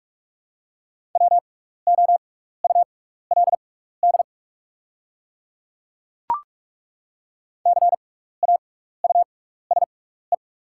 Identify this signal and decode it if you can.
Morse code, heard as a keyed tone, about 48 wpm, spaced as for 16 wpm, 690 Hz. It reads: WOULD CAUSE
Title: 6x Character Spacing - Top 200 Words - 45wpm